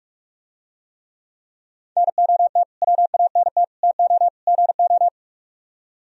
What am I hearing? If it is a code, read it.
Morse code, heard as a keyed tone, about 45 wpm, 700 Hz. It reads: NOT WANT TO GO